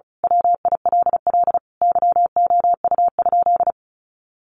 Morse code, heard as a keyed tone, about 34 wpm, 700 Hz. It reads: WILL YOU?
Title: Sentences from Top 1400 Words 35wpm